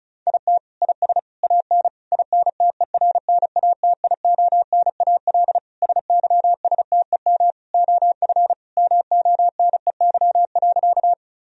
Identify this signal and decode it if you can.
Morse code, heard as a keyed tone, about 34 wpm, 700 Hz. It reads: IT IS AN INTERNATIONAL SYSTEM OF MONEY.
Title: Sentences from Top 1000 Words - Review Part 4 - 35wpm